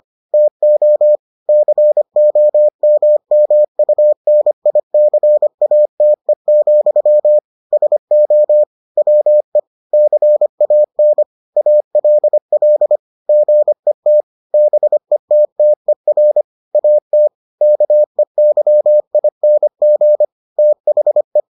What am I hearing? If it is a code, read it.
Morse code, heard as a keyed tone, about 25 wpm, 600 Hz. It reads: TO COMMUNICATE, SO WE CAN ALL GET BETTER AT KEYING THE